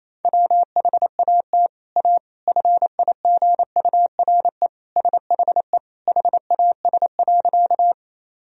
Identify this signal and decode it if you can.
Morse code, heard as a keyed tone, about 27 wpm, 700 Hz. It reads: WHAT A FIGURE SHE HAS.
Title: Sentences from Top 1000 Words - Review Part 2 - 28wpm